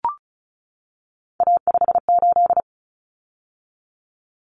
Morse code, heard as a keyed tone, about 34 wpm, 700 Hz. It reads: A58
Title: Common Cut Numbers - 3-Digit 35wpm